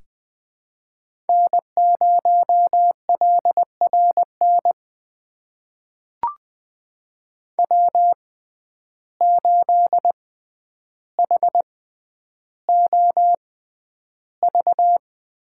Morse code, heard as a keyed tone, about 21 wpm, spaced as for 10 wpm, 700 Hz. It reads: N0LRN W8HOV
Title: USA Call Signs - 5 Characters - 6x Character Spacing - 20wpm